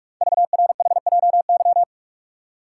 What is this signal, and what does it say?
Morse code, heard as a keyed tone, about 44 wpm, 690 Hz. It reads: VRHJY
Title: Code Groups - 5 Letter:Numbers - 45wpm